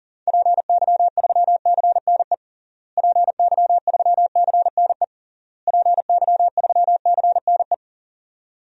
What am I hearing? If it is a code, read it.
Morse code, heard as a keyed tone, about 37 wpm, 700 Hz. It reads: PY3CNE PY3CNE PY3CNE
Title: DXCC-100 Call Signs - 6 Characters - Repeated 3x - 40wpm